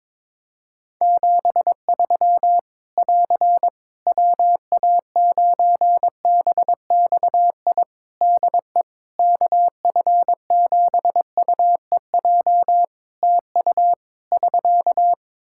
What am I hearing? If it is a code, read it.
Morse code, heard as a keyed tone, about 22 wpm, 700 Hz. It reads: 73 <AR> WA9BXI DE KF7UEJ TU <SK>
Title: K1USN SST QSO Practice 22wpm